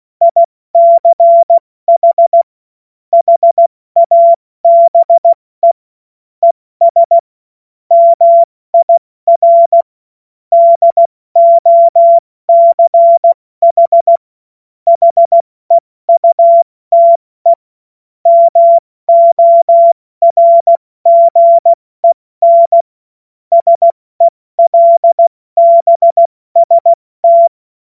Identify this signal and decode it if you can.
Morse code, heard as a keyed tone, about 16 wpm, spaced as for 14 wpm, 680 Hz. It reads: ICH HABE ES MIR DOCH HEUTE MORGEN SELBST
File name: Morsekurs / Witze / Witz 01. November 2022 70 F80